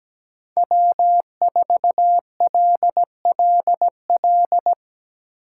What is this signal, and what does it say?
Morse code, heard as a keyed tone, about 17 wpm, 700 Hz. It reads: W4LLL
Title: USA Call Signs - 5 Characters - 8x Character Spacing - 17wpm